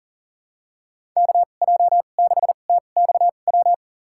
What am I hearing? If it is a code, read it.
Morse code, heard as a keyed tone, about 40 wpm, spaced as for 29 wpm, 700 Hz. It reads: KJ6TXW
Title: USA Call Signs - 6 Characters - 2x Character Spacing - 40wpm